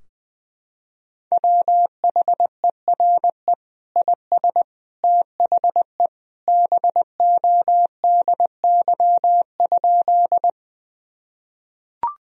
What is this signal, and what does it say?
Morse code, heard as a keyed tone, about 20 wpm, 700 Hz. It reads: WHERE IS THE BODY?
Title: Sentences from Top 1000 Words - Review Part 2 - 20wpm